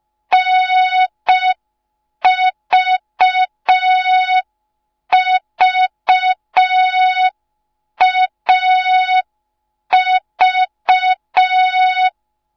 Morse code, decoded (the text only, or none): NVVAV